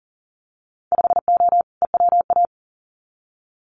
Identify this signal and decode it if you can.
Morse code, heard as a keyed tone, about 40 wpm, 700 Hz. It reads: 5O EWA
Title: Winter Field Day Exchanges Encore 40wpm